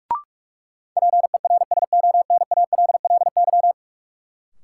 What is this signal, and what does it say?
Morse code, heard as a keyed tone, about 42 wpm, 700 Hz. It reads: PERSONALLY
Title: Sentences from Top 1400 Words - New Word Drill Alt - 45wpm